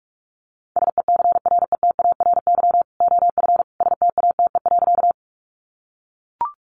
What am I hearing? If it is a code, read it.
Morse code, heard as a keyed tone, about 44 wpm, 700 Hz. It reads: SECRETARY OF STATE.
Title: Sentences from Top 1300 Words 45wpm